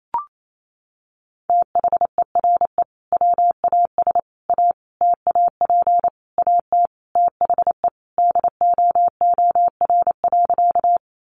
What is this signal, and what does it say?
Morse code, heard as a keyed tone, about 28 wpm, 700 Hz. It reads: THERE WAS A TAP AT THE DOOR.
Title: Sentences from Top 1400 Words 28wpm